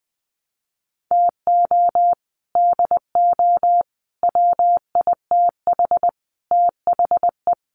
Morse code, heard as a keyed tone, about 20 wpm, 700 Hz. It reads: TO DO WITH THE
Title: Sets of 4 Words 20wpm